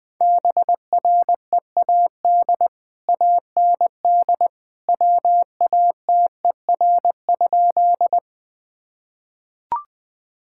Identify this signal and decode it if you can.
Morse code, heard as a keyed tone, about 20 wpm, 700 Hz. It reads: BREAD AND WATER?
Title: Sentences from Top 1600 Words 20wpm